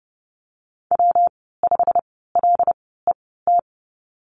Morse code, heard as a keyed tone, about 30 wpm, spaced as for 17 wpm, 700 Hz. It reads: W5LET
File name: Lesson 19g - Letter w - Call Signs 30wpm